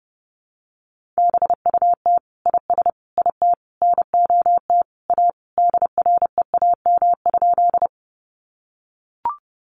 Morse code, heard as a keyed tone, about 30 wpm, 700 Hz. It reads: BUT IS IT NOT A DREAM?
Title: Sentences from Top 1500 Words 30wpm